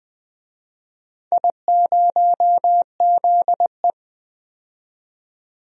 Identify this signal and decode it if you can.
Morse code, heard as a keyed tone, about 20 wpm, 700 Hz. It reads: I0ZE
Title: DXCC-100 Call Signs - 4 Characters - 20wpm